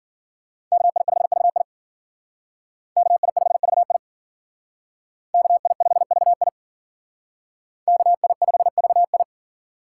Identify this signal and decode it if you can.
Morse code, heard as a keyed tone, about 51 wpm, 700 Hz. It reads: KI5VI KI5VI KI5VI KI5VI